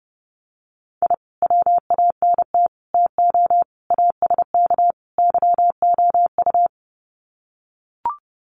Morse code, heard as a keyed tone, about 29 wpm, 700 Hz. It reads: I WANT TO ASK YOU
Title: Sets of 5 Words 30wpm